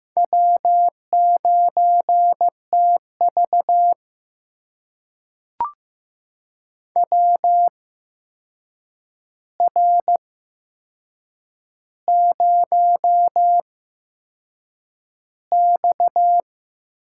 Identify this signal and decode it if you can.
Morse code, heard as a keyed tone, about 15 wpm, spaced as for 6 wpm, 700 Hz. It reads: W9TV WR0X
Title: USA Call Signs - 4 Characters - 8x Character Spacing - 15wpm